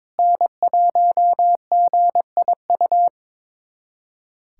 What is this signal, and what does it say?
Morse code, heard as a keyed tone, about 21 wpm, 700 Hz. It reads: N1GIU
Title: USA Callsign DE Callsign - 5 Characters - 22wpm